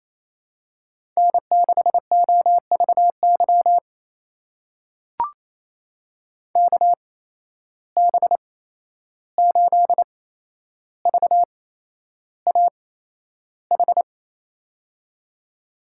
Morse code, heard as a keyed tone, about 28 wpm, spaced as for 11 wpm, 700 Hz. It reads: N6OVY KB8VAH
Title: USA Call Signs - 8x Character Spacing - 28wpm